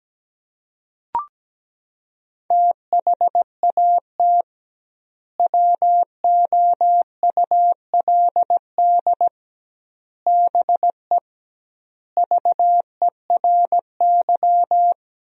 Morse code, heard as a keyed tone, about 17 wpm, 700 Hz. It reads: THAT WOULD BE VERY
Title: 2x Word-Spacing - Sentences from Top 200 Words - 17wpm